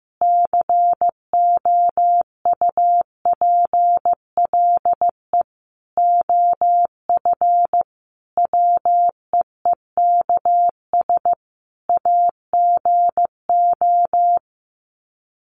Morse code, heard as a keyed tone, about 15 wpm, 700 Hz. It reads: COUPLE OF WEEKS AGO
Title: Sets of 5 Words 15wpm